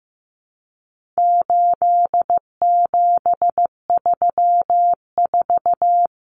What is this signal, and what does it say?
Morse code, heard as a keyed tone, about 15 wpm, 700 Hz. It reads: 8734
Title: Common Cut Numbers - 4-Digit 15wpm